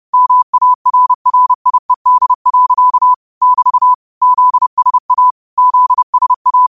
Morse code, heard as a keyed tone, about 30 wpm, 1000 Hz. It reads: MARRIED. <BT> ZSA ZSA